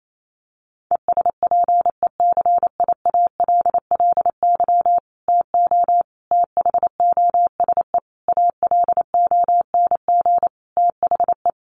Morse code, heard as a keyed tone, about 28 wpm, 700 Hz. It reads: ESPECIALLY TO THOSE ALONG THE
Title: Sentences from Top 1500 Words 28wpm